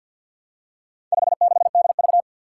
Morse code, heard as a keyed tone, about 46 wpm, 700 Hz. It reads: 56DV
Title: Four Challenging Letter-Number Combinations 50wpm